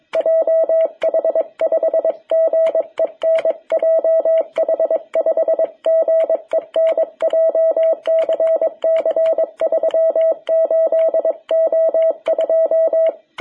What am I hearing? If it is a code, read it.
Morse code, heard as a keyed tone, about 22 wpm, 660 Hz. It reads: JH5GEN JH5GEN JCC 38O2
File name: jh8cbh / 2021cw